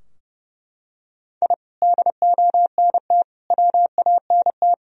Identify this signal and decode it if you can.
Morse code, heard as a keyed tone, about 32 wpm, 700 Hz. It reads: I DONT WANT
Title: Sentences from Top 1000 Words - Review Part 3 - 30wpm